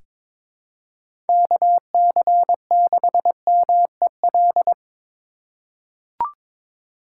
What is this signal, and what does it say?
Morse code, heard as a keyed tone, about 22 wpm, 700 Hz. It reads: KC6MEL
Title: USA Call Signs - 6 Characters - 4x Character Spacing - 22wpm